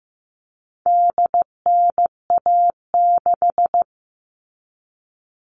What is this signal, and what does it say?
Morse code, heard as a keyed tone, about 15 wpm, 700 Hz. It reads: DNA6
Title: All Cut Numbers - 4-Digit 15wpm